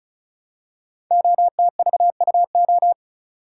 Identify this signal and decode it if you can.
Morse code, heard as a keyed tone, about 36 wpm, 700 Hz. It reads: OTVUO
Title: Code Groups - 5 Letter:Numbers - 35wpm